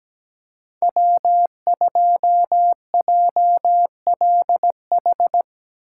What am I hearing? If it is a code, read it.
Morse code, heard as a keyed tone, about 17 wpm, 700 Hz. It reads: W2JLH
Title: USA Call Signs - 5 Characters - 4x Character Spacing - 17wpm